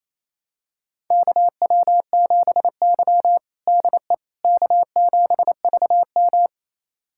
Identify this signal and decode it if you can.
Morse code, heard as a keyed tone, about 28 wpm, 700 Hz. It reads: KW7Y DE K7VM
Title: USA Callsign DE Callsign - 4 Characters - 28wpm